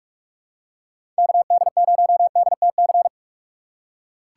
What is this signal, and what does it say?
Morse code, heard as a keyed tone, about 43 wpm, 700 Hz. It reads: KD0DTC
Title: USA Call Signs - 6 Characters - 2x Character Spacing - 45wpm